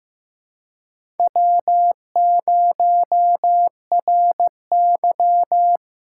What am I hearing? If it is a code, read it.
Morse code, heard as a keyed tone, about 15 wpm, 700 Hz. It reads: W0RY
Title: USA Call Signs - 4 Characters - 2x Character Spacing - 15wpm